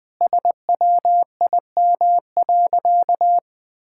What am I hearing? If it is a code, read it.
Morse code, heard as a keyed tone, about 20 wpm, 700 Hz. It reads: SWIM.
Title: Sentences with 5-Word Phrases - Spoken Beginning - 20wpm